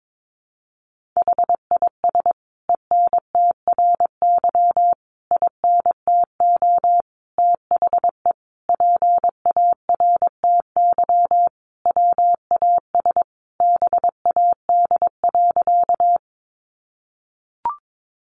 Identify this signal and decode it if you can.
Morse code, heard as a keyed tone, about 22 wpm, 700 Hz. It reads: HIS ENTRY INTO THE PARTY WAS BAD.
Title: Sentences from Top 1500 Words 22wpm